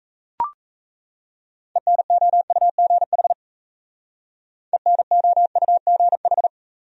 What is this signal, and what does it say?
Morse code, heard as a keyed tone, about 37 wpm, 690 Hz. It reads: ENOUGH ENOUGH